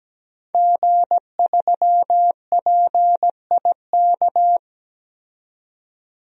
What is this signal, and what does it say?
Morse code, heard as a keyed tone, about 17 wpm, 700 Hz. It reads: G3PIK
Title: DXCC-100 Call Signs - 5 Characters - 17wpm